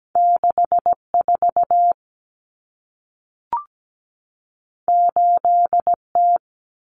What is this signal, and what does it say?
Morse code, heard as a keyed tone, about 17 wpm, 700 Hz. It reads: 64 8T
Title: Common Cut Numbers - 2-Digit 17wpm